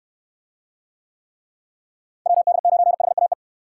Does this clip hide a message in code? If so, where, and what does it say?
Yes: Morse code from 2.3 to 3.3 s, heard as a keyed tone, about 68 wpm, 700 Hz. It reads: WD9HAE